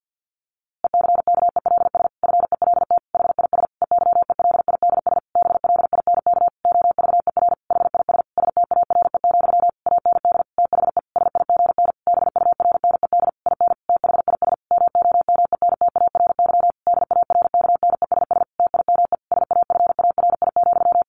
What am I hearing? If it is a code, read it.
Morse code, heard as a keyed tone, about 50 wpm, 700 Hz. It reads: ECKELS FELT HIS EYELIDS BLINK OVER HIS STARE, AND THE SIGN BURNED IN THIS MOMENTARY DARKNESS TIME SAFARI,
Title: A Sound Of Thunder 50wpm